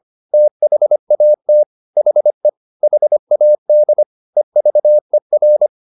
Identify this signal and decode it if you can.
Morse code, heard as a keyed tone, about 25 wpm, 600 Hz. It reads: THAT HE HAD EVER
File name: the old man and the sea part 1 25wpm